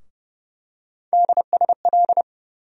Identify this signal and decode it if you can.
Morse code, heard as a keyed tone, about 30 wpm, 700 Hz. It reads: DSL
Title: USA Call Sign Postfixes - 3 Letters 30wpm